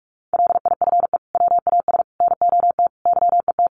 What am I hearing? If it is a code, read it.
Morse code, heard as a keyed tone, about 44 wpm, 700 Hz. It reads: LIFE WAS NOT YET
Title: Sentences from Top 1200 Words 45wpm